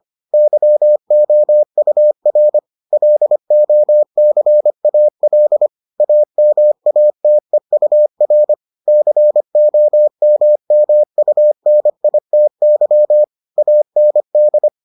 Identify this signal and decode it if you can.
Morse code, heard as a keyed tone, about 25 wpm, 600 Hz. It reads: YOUR LOCAL AMATEUR COMMUNITY AND